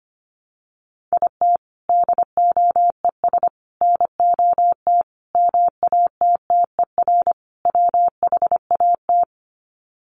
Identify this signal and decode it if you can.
Morse code, heard as a keyed tone, about 25 wpm, 700 Hz. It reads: IT DOES NOT MATTER WHAT
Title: Sets of 5 Words Encore 25wpm